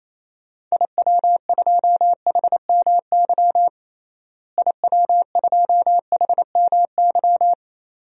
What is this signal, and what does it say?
Morse code, heard as a keyed tone, about 27 wpm, 700 Hz. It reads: IW2HMY IW2HMY
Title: DXCC-100 Call Signs - 6 Characters - Repeated 2x - 28wpm